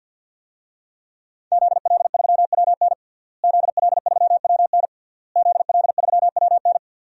Morse code, heard as a keyed tone, about 50 wpm, 700 Hz. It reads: ZL3WN ZL3WN ZL3WN
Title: DXCC-100 Call Signs - 5 Characters - Repeated 3x - 50wpm